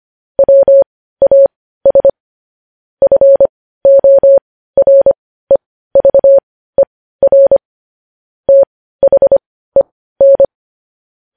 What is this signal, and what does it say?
Morse code, heard as a keyed tone, about 25 wpm, spaced as for 15 wpm, 550 Hz. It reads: WAS FOREVER THEN
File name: podcast / News headlines-1x-25x15